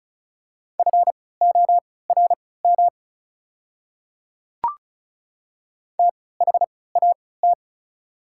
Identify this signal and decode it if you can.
Morse code, heard as a keyed tone, about 34 wpm, spaced as for 20 wpm, 700 Hz. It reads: FORM THAT
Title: 3x Character Spacing - Top 300 Words - 35wpm